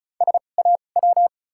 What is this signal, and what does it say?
Morse code, heard as a keyed tone, about 36 wpm, spaced as for 25 wpm, 700 Hz. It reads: SAW
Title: 2x Character Spacing - Top 300 Words - 35wpm